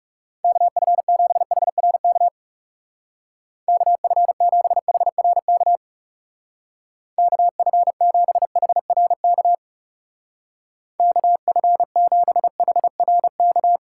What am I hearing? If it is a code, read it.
Morse code, heard as a keyed tone, about 38 wpm, 700 Hz. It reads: KF7HRK KF7HRK KF7HRK KF7HRK
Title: Speed Racing - USA 2x3 Call Signs 30wpm